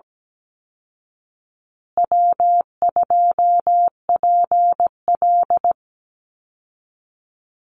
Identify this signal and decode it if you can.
Morse code, heard as a keyed tone, about 17 wpm, 700 Hz. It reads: W2PL
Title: Easy Call Signs Repeated 3x - 17wpm